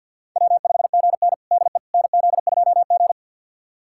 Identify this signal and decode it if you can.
Morse code, heard as a keyed tone, about 52 wpm, 700 Hz. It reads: W5GN DE NZ2G
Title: USA Callsign DE Callsign - 4 Characters - 50wpm